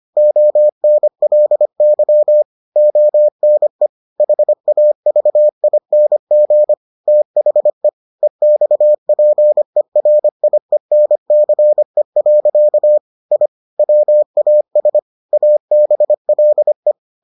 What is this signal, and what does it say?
Morse code, heard as a keyed tone, about 25 wpm, 600 Hz. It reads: ONLY ONE HAVING THE EXPERIENCE. I WAS ABLE